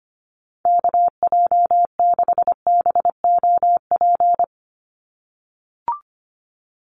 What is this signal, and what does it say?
Morse code, heard as a keyed tone, about 25 wpm, 700 Hz. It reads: KJ6BOP
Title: USA Call Signs 25wpm